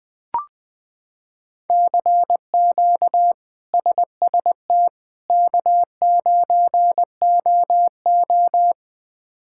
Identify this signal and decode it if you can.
Morse code, heard as a keyed tone, about 21 wpm, 700 Hz. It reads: CQ SST K9OO
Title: K1USN SST QSO Practice - 20wpm